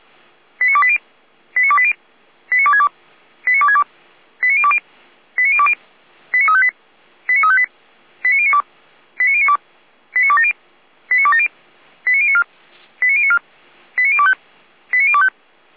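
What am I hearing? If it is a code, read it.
Five-tone selective call (ZVEI), heard as quick runs of five tones, tones of about 80 ms. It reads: 89280 89280 89262 89262 89020 89020 89358 89358 89092 89092 89270 89270 89094 89094 89025 89025
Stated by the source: ZVEI five-tone selective call 89330, received by SDR.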